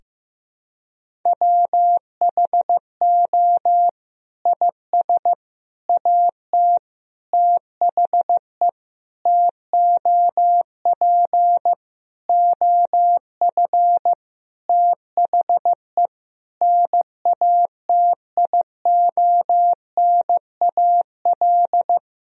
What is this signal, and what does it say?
Morse code, heard as a keyed tone, about 15 wpm, 700 Hz. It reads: WHO IS AT THE TOP OF THE NATIONAL
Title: Sentences from Top 1000 Words - Review Part 4 - 15wpm